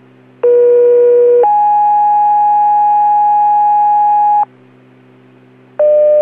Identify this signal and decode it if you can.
Two-tone page, heard as a one-second tone, then a three-second tone, about 480 then 830 Hz. It is heard from 0.4 to 4.4 s.